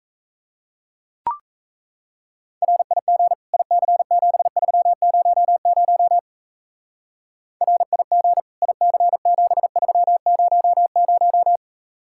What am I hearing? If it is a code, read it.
Morse code, heard as a keyed tone, about 35 wpm, 700 Hz. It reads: RIG IC7300 RIG IC7300
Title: Speed-Racing Common QSO Phrases - 28wpm